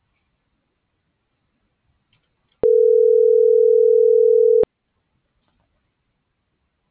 Ringback tone (North America), 440 + 480 Hz.